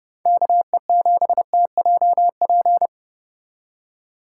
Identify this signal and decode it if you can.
Morse code, heard as a keyed tone, about 29 wpm, 700 Hz. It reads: KE7TJP